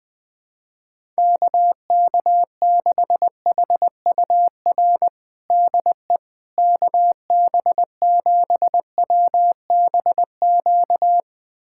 Morse code, heard as a keyed tone, about 21 wpm, 700 Hz. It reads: KK6HUR DE KB7WBQ
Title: USA Callsign DE Callsign - 6 Characters - 20wpm